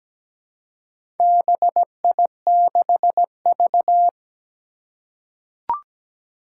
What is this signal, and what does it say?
Morse code, heard as a keyed tone, about 17 wpm, 700 Hz. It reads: BI6V
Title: Four Challenging Letter-Number Combinations 17wpm